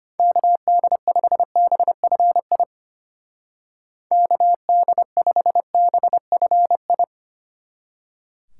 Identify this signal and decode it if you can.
Morse code, heard as a keyed tone, about 31 wpm, 700 Hz. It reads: KD5BFI KD5BFI
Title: Speed Racing - USA 2x3 Call Signs 25wpm